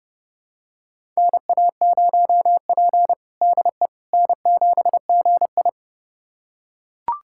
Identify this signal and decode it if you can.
Morse code, heard as a keyed tone, about 30 wpm, 700 Hz. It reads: NA0P DE N7GI
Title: USA Callsign DE Callsign - 4 Characters - 30wpm